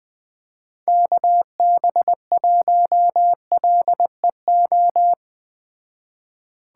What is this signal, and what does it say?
Morse code, heard as a keyed tone, about 19 wpm, 700 Hz. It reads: KB1LEO
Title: Speed Racing - USA 2x3 Call Signs 17wpm